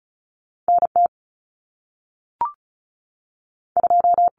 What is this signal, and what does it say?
Morse code, heard as a keyed tone, about 36 wpm, 700 Hz. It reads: NT 2
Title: Common Cut Numbers - 2-Digit 35wpm